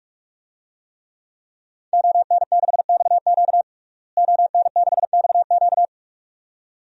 Morse code, heard as a keyed tone, about 46 wpm, 700 Hz. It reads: ON6XQ ON6XQ
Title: DXCC-100 Call Signs - 5 Characters - Repeated 2x - 45wpm